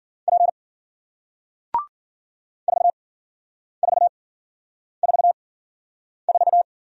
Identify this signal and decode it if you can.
Morse code, heard as a keyed tone, about 53 wpm, spaced as for 9 wpm, 700 Hz. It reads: P 4444